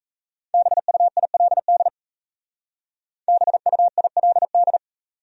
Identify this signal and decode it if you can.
Morse code, heard as a keyed tone, about 34 wpm, 700 Hz. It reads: BUILD BUILD
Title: Speed-Racing Top 500 Words 28wpm